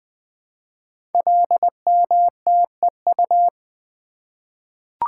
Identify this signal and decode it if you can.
Morse code, heard as a keyed tone, about 20 wpm, 700 Hz. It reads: LMTEU
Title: Code Groups - 5 Letters - 20wpm